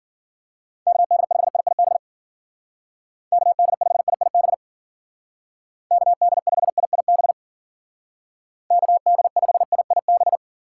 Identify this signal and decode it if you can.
Morse code, heard as a keyed tone, about 49 wpm, 700 Hz. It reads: KD5IIB KD5IIB KD5IIB KD5IIB
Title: Speed Racing - USA 2x3 Call Signs 40wpm